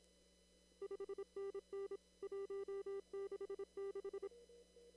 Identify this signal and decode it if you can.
Morse code, heard as a keyed tone, about 26 wpm, 400 Hz. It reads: 5NN 166
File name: CW / kt5j wpx cw-op-k5pi